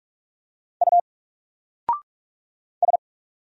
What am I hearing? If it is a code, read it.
Morse code, heard as a keyed tone, about 46 wpm, 700 Hz. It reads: U S
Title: Single Letter-Number 45wpm